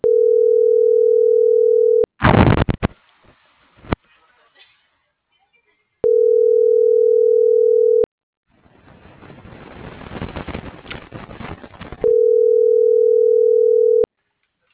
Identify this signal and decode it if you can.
Ringback tone (North America), 440 + 480 Hz.